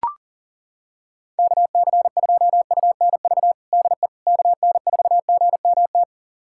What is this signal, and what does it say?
Morse code, heard as a keyed tone, about 40 wpm, 700 Hz. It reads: KC2UNV DE KN4GMT
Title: USA Callsign DE Callsign - 6 Characters - 40wpm